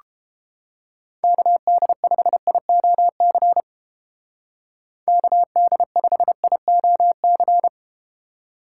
Morse code, heard as a keyed tone, about 31 wpm, 700 Hz. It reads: KD5IOC KD5IOC